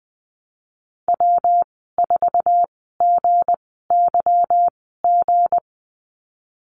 Morse code, heard as a keyed tone, about 20 wpm, spaced as for 15 wpm, 700 Hz. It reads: W4GYG
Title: 161 Lesson 26g - Letter g - Call Signs 20wpm